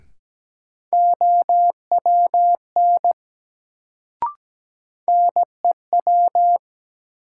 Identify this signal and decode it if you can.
Morse code, heard as a keyed tone, about 17 wpm, 700 Hz. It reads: OWN NEW
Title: Sets of 2 Words - New Word Drill - 17wpm